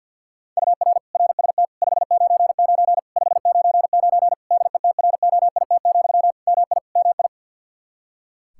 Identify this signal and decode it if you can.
Morse code, heard as a keyed tone, about 49 wpm, 700 Hz. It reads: UR RST 599 599 DETROIT, MI MI
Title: Practice QSO - Exchange 1 - Beginning - 50wpm